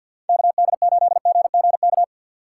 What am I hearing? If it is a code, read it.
Morse code, heard as a keyed tone, about 52 wpm, 690 Hz. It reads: KD8GGK
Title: USA Call Signs - 6 Characters - 3x Character Spacing - 50wpm